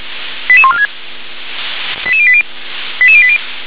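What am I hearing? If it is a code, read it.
Five-tone selective call (ZVEI), heard as quick runs of five tones, tones of about 70 ms. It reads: 88146 88080 88080
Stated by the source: ZVEI five-tone selective call received by SDR.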